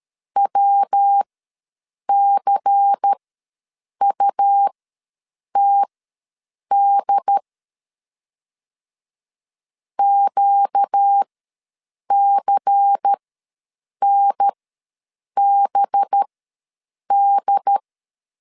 Morse code, heard as a keyed tone, about 13 wpm, spaced as for 7 wpm, 800 Hz. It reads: WCUTD QCNBD
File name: LEZIONI / 5 LBVC / 5 LBVC ws 12wpm 3159